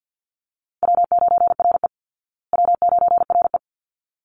Morse code, heard as a keyed tone, about 50 wpm, 700 Hz. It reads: W9RE W9RE
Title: Easy Call Signs Repeated 2x - 50wpm